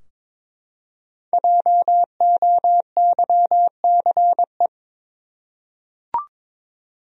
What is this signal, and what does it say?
Morse code, heard as a keyed tone, about 22 wpm, 700 Hz. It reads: JOYCE
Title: USA Common Names - 22wpm